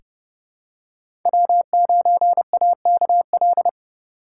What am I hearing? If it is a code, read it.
Morse code, heard as a keyed tone, about 30 wpm, 700 Hz. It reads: W9AKL
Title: USA Call Signs - 5 Characters - 3x Character Spacing - 30wpm